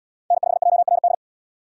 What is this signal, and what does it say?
Morse code, heard as a keyed tone, about 66 wpm, 700 Hz. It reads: N5JRD